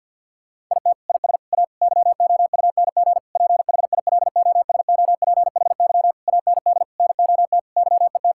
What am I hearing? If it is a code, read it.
Morse code, heard as a keyed tone, about 50 wpm, 700 Hz. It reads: IT IS A YOUNG PHILOSOPHY AND NOT YET
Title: Sentences from Top 1000 Words - Review Part 6 - 50wpm